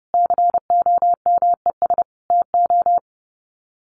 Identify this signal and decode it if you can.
Morse code, heard as a keyed tone, about 30 wpm, 700 Hz. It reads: COMES TO
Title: Sets of 3 Words 30wpm